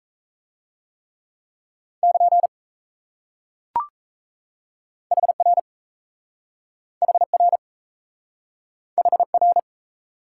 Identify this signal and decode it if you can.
Morse code, heard as a keyed tone, about 37 wpm, 700 Hz. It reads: <KN> HR HR HR